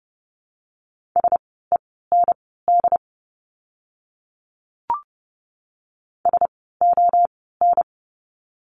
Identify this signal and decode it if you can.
Morse code, heard as a keyed tone, about 30 wpm, spaced as for 17 wpm, 700 Hz. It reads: SEND SON
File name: Lesson 17f - Letter u - Words 30wpm